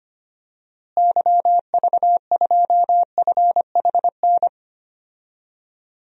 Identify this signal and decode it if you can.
Morse code, heard as a keyed tone, about 25 wpm, 700 Hz. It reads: YV2FHN
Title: DXCC-100 Call Signs - 6 Characters - Repeated 3x - 25wpm